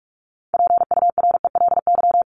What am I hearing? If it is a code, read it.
Morse code, heard as a keyed tone, about 44 wpm, 700 Hz. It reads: PURELY
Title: LY Suffix - Top 100 Words with Base Words 45wpm